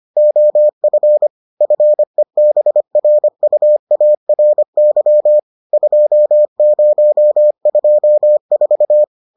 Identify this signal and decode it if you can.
Morse code, heard as a keyed tone, about 25 wpm, 600 Hz. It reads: OF FEBRUARY 2024